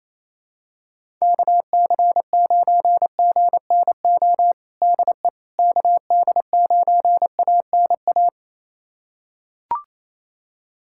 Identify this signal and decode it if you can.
Morse code, heard as a keyed tone, about 27 wpm, 700 Hz. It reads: KC9GNO DE KD9ANA
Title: USA Callsign DE Callsign - 6 Characters - 28wpm